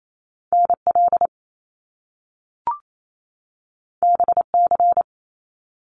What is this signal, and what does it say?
Morse code, heard as a keyed tone, about 28 wpm, 700 Hz. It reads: NL BC
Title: Canadian Provinces and Territories 28wpm